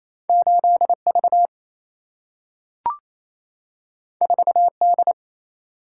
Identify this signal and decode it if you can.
Morse code, heard as a keyed tone, about 28 wpm, 700 Hz. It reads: 8V 4D
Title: Single-Letter Single-Number 28wpm